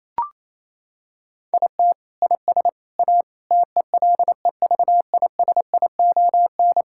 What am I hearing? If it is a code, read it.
Morse code, heard as a keyed tone, about 28 wpm, 700 Hz. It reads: IT IS A TELEVISION
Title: Sentences from Top 1000 Words - Review Part 5 - 28wpm